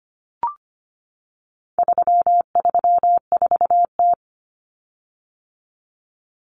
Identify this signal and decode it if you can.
Morse code, heard as a keyed tone, about 25 wpm, 700 Hz. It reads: 334T
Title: Common Cut Numbers - 4-Digit 25wpm